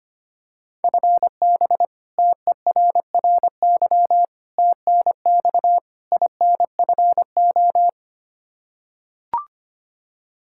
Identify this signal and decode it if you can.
Morse code, heard as a keyed tone, about 25 wpm, 700 Hz. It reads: FB TERRY TNX INFO
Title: Practice QSO - Exchange 3 - More Info - 25wpm